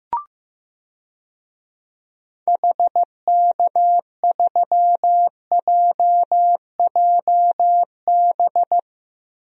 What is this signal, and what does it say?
Morse code, heard as a keyed tone, about 16 wpm, 700 Hz. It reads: HK3JJB
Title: DXCC-100 Call Signs - 6 Characters - Repeated 2x - 15wpm